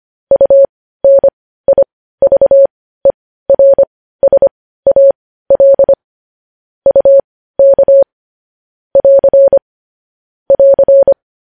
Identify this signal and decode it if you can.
Morse code, heard as a keyed tone, about 25 wpm, spaced as for 15 wpm, 550 Hz. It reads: UNIVERSAL UK <AR> <AR>
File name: podcast / News headlines-1x-25x15